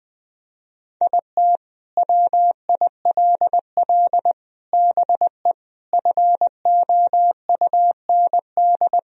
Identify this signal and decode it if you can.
Morse code, heard as a keyed tone, about 21 wpm, 700 Hz. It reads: IT WILL BE FOUND
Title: Sentences from Top 1000 Words - Review Part 1 - 20wpm